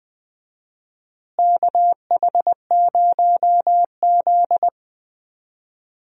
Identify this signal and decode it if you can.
Morse code, heard as a keyed tone, about 21 wpm, 700 Hz. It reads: KH0Z
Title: USA Call Signs - 4 Characters - Repeated 2x - 20wpm